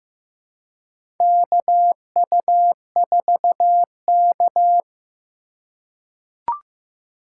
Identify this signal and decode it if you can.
Morse code, heard as a keyed tone, about 15 wpm, 700 Hz. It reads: KU4K
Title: USA Call Signs - 4 Characters - 3x Character Spacing - 15wpm